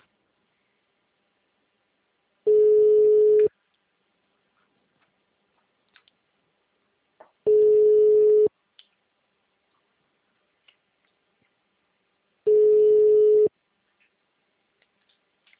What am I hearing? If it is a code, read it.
Ringback tone (Europe, 425 Hz), with speech.